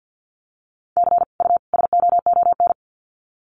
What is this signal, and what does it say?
Morse code, heard as a keyed tone, about 50 wpm, 700 Hz. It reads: C U SOON
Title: Common QSO Phrases 50wpm